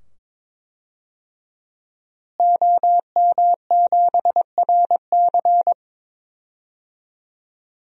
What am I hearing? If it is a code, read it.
Morse code, heard as a keyed tone, about 22 wpm, 700 Hz. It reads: OM7RC
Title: DXCC-100 Call Signs - 5 Characters - Repeated 2x - 22wpm